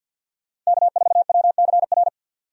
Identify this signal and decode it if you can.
Morse code, heard as a keyed tone, about 46 wpm, 700 Hz. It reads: K4WCR
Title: USA Call Signs - 5 Characters - 50wpm